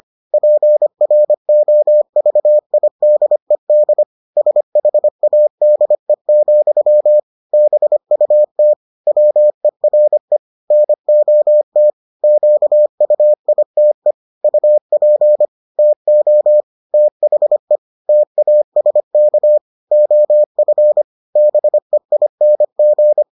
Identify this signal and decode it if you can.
Morse code, heard as a keyed tone, about 25 wpm, 600 Hz. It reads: PROVIDED SHADE, BUT WERE NOT QUITE UP TO THE TASK OF BEING